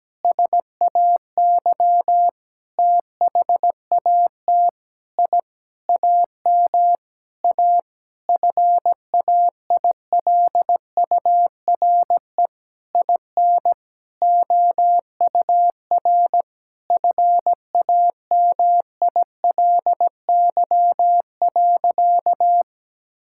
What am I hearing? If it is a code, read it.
Morse code, heard as a keyed tone, about 17 wpm, 700 Hz. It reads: SAY THAT I AM A FAILURE IN OUR FAMILY.
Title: Sentences from Top 1000 Words - Review Part 6 - 17wpm